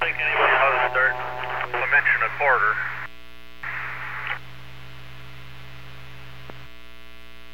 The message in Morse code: none

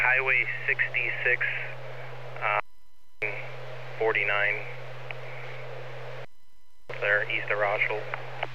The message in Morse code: none

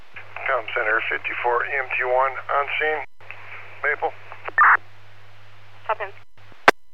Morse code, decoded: none

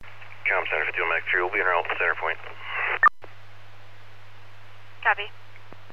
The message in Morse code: none

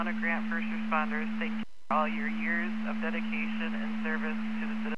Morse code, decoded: none

